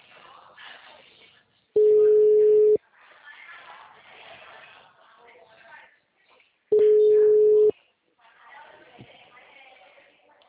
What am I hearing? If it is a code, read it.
Ringback tone (Europe, 425 Hz), with speech.
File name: out- -2003- - - .39601